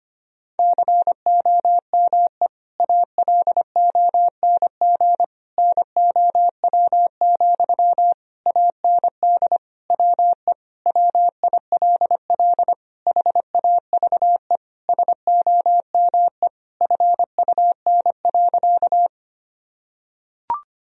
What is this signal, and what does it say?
Morse code, heard as a keyed tone, about 25 wpm, 690 Hz. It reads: COME ALONG NOW, AND WE WILL HAVE SOME FUN.
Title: Sentences from Top 1000 Words - Review Part 4 - 25wpm